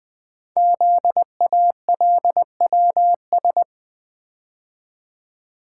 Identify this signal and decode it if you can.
Morse code, heard as a keyed tone, about 20 wpm, 700 Hz. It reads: ZALWS
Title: Code Groups - 5 Letters - 20wpm